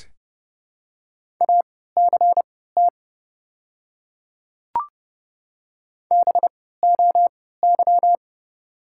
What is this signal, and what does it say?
Morse code, heard as a keyed tone, about 30 wpm, spaced as for 17 wpm, 690 Hz. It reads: ACT BOY
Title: 3x Character Spacing - Top 200 Words - 30wpm